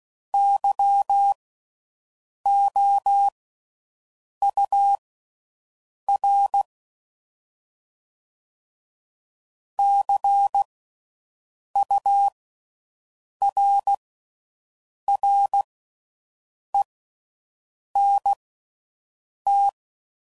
Morse code, heard as a keyed tone, about 16 wpm, spaced as for 6 wpm, 800 Hz. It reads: YOUR CURRENT